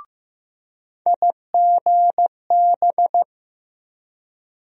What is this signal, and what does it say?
Morse code, heard as a keyed tone, about 15 wpm, 700 Hz. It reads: IGB
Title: USA Call Sign Postfixes - 3 Letters 15wpm